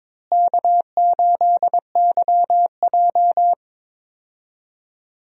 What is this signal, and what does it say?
Morse code, heard as a keyed tone, about 22 wpm, 700 Hz. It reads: K8YJ
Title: USA Call Signs - 4 Characters - Repeated 3x - 22wpm